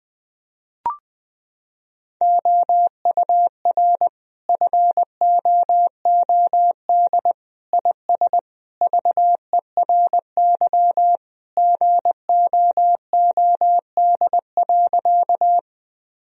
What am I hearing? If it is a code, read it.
Morse code, heard as a keyed tone, about 21 wpm, 700 Hz. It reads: OUR FOOD IS VERY GOOD.
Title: Sentences from Top 1000 Words - Review Part 1 - 20wpm